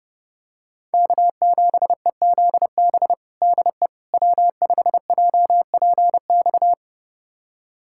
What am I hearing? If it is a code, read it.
Morse code, heard as a keyed tone, about 30 wpm, 700 Hz. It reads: K7EZB DE W5JPX
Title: USA Callsign DE Callsign - 5 Characters - 30wpm